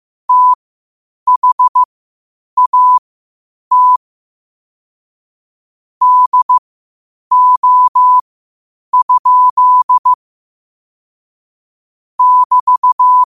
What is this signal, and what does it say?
Morse code, heard as a keyed tone, about 15 wpm, spaced as for 8 wpm, 1000 Hz. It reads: THAT DO? <BT>
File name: qotdfiles / 8wpm